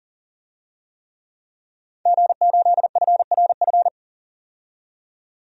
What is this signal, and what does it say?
Morse code, heard as a keyed tone, about 40 wpm, 700 Hz. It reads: G8FRF
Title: DXCC-100 Call Signs - 5 Characters - 40wpm